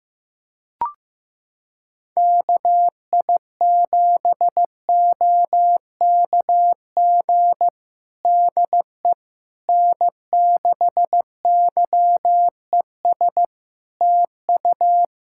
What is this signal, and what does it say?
Morse code, heard as a keyed tone, about 15 wpm, 700 Hz. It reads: KI7OKG DE N6YES TU
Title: Practice QSO - Exchange 4 - Ending - 15wpm